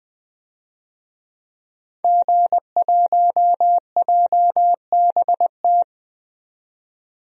Morse code, heard as a keyed tone, about 21 wpm, 700 Hz. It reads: G1JBT
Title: DXCC-100 Call Signs - 5 Characters - 20wpm